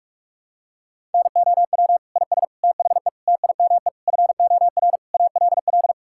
Morse code, heard as a keyed tone, about 46 wpm, 700 Hz. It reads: NOW IS THE TIME FOR ALL
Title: Pangram Challenge v2 - 45wpm